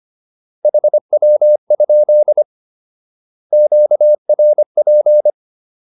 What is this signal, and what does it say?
Morse code, heard as a keyed tone, about 25 wpm, 600 Hz. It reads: HW? QRP